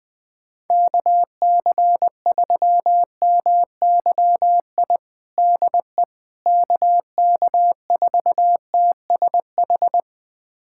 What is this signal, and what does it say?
Morse code, heard as a keyed tone, about 20 wpm, 700 Hz. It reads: KC3MYI DE KK4TSH
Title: USA Callsign DE Callsign - 6 Characters - 20wpm